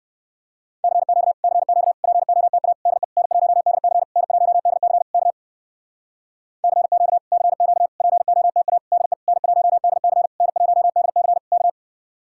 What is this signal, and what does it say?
Morse code, heard as a keyed tone, about 59 wpm, 700 Hz. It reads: CQ CQ POTA DE N1DC N1DC K CQ CQ POTA DE N1DC N1DC K